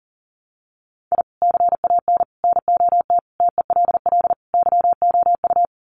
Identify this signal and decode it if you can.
Morse code, heard as a keyed tone, about 40 wpm, 700 Hz. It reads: I CAN NOT TELL YOU
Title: Sets of 5 Words 40wpm